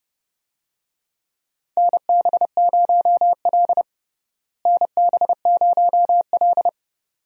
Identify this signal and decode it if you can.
Morse code, heard as a keyed tone, about 30 wpm, 700 Hz. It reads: NB0L NB0L
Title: USA Call Signs - 4 Characters - Repeated 2x - 30wpm